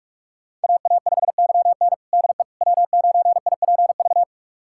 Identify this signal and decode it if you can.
Morse code, heard as a keyed tone, about 46 wpm, 700 Hz. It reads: AA5YN DE W9IPV
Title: USA Callsign DE Callsign - 5 Characters - 45wpm